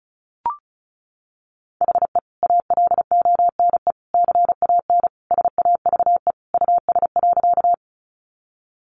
Morse code, heard as a keyed tone, about 34 wpm, 700 Hz. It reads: HE ALONE CAN SAVE US.
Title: Sentences from Top 1100 Words 35wpm